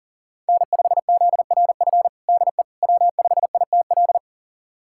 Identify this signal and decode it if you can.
Morse code, heard as a keyed tone, about 40 wpm, 700 Hz. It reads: N5ZRF DE W5ITL
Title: USA Callsign DE Callsign - 5 Characters - 40wpm